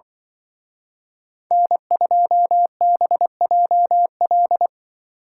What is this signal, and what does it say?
Morse code, heard as a keyed tone, about 24 wpm, 700 Hz. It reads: N2BJL
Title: Speed Racing - USA Call Signs 20wpm